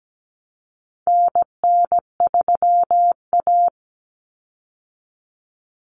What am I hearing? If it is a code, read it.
Morse code, heard as a keyed tone, about 17 wpm, 700 Hz. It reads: NN3A
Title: Common Cut Numbers - 4-Digit 17wpm